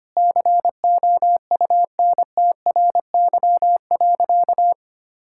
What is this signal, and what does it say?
Morse code, heard as a keyed tone, about 25 wpm, 700 Hz. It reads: COUNTRY.
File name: practice / 25wpm / Sentences with 9-Word Phrases - 25wpm